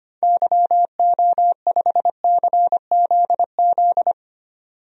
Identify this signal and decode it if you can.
Morse code, heard as a keyed tone, about 25 wpm, 700 Hz. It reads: YO5CZZ
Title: DXCC-100 Call Signs - 6 Characters - 25wpm